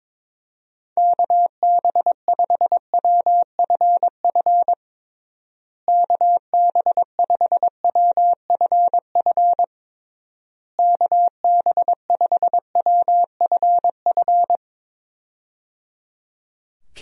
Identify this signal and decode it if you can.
Morse code, heard as a keyed tone, about 22 wpm, 700 Hz. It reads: KB5WFF KB5WFF KB5WFF
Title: USA Call Signs - 6 Characters - Repeated 3x - 22wpm